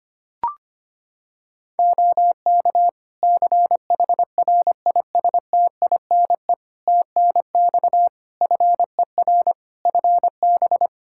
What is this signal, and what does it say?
Morse code, heard as a keyed tone, about 25 wpm, 700 Hz. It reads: OK CHRISTINE TNX FER FB
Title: Practice QSO - Exchange 4 - Ending - 25wpm